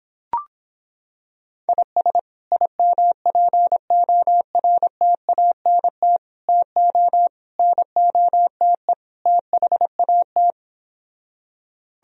Morse code, heard as a keyed tone, about 26 wpm, 700 Hz. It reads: IS IMPORTANT TO NOTE THAT